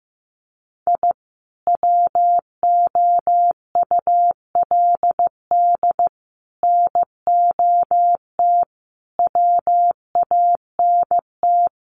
Morse code, heard as a keyed tone, about 15 wpm, 700 Hz. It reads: I WOULD NOT WANT
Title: Sets of 5 Words 15wpm